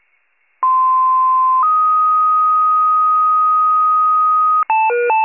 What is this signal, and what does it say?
Two-tone page, heard as a one-second tone, then a three-second tone, about 1040 then 1250 Hz. It is heard from 0.6 to 4.6 s.